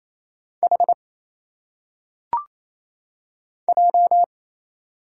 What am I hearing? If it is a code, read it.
Morse code, heard as a keyed tone, about 27 wpm, 700 Hz. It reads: H J
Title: Single Letters - Phonetic Pronunciation 28wpm